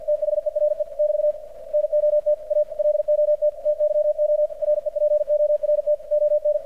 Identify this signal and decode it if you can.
Morse code, heard as a keyed tone, about 53 wpm, 600 Hz, with radio noise. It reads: THERE <BT> NOT A LOT TYO REPORT ON
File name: archived / CRSnet